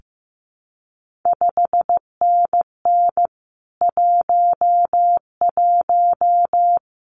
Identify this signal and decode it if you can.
Morse code, heard as a keyed tone, about 15 wpm, 700 Hz. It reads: 5NN 11
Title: CQ WW DX Contest Exchange Encore - 15wpm